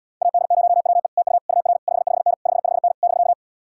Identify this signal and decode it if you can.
Morse code, heard as a keyed tone, about 79 wpm, 700 Hz. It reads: AD0WE TU UR 55N 55N <BK>